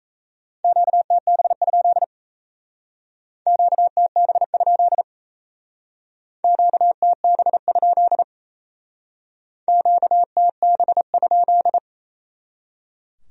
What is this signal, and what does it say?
Morse code, heard as a keyed tone, about 35 wpm, 700 Hz. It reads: QTB? QTB? QTB? QTB?
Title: Speed-Racing Q-Codes - 28wpm